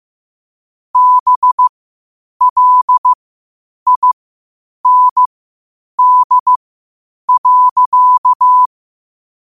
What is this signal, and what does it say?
Morse code, heard as a keyed tone, about 15 wpm, spaced as for 8 wpm, 1000 Hz. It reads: BLIND.
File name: qotdfiles / 8wpm